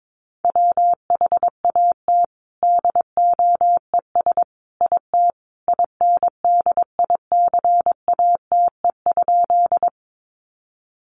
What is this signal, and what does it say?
Morse code, heard as a keyed tone, about 22 wpm, 700 Hz. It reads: WHAT DOES IT INDICATE?
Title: Sentences from Top 1300 Words 22wpm